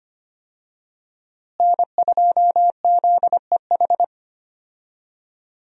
Morse code, heard as a keyed tone, about 25 wpm, 700 Hz. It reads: N2ZEH
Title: USA Call Signs - 5 Characters - Repeated 2x - 25wpm